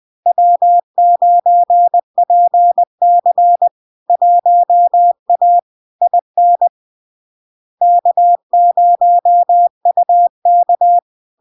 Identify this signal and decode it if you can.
Morse code, heard as a keyed tone, about 20 wpm, 700 Hz. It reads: W9PC 1A IN K0UK
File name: LARK-FD-2022-Log-in-CW-20wpm-700-Hz